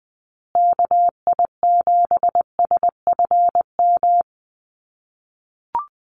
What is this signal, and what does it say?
Morse code, heard as a keyed tone, about 20 wpm, 700 Hz. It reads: KI7SFM
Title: USA Call Signs Repeated 3x 20wpm